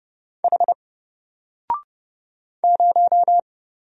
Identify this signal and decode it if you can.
Morse code, heard as a keyed tone, about 30 wpm, 700 Hz. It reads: H 0
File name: Single-Letter-Number-30wpm